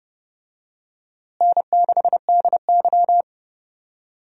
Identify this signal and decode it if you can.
Morse code, heard as a keyed tone, about 31 wpm, 700 Hz. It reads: N6DY
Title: USA Call Signs - 4 Characters - Repeated 3x - 30wpm